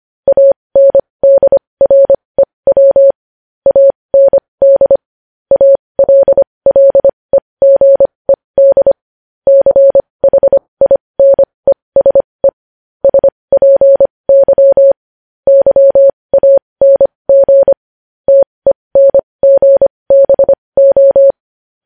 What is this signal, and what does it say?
Morse code, heard as a keyed tone, about 25 wpm, spaced as for 20 wpm, 550 Hz. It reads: ANDREW AND ALLEGED CHINESE SPY YANG TENGBO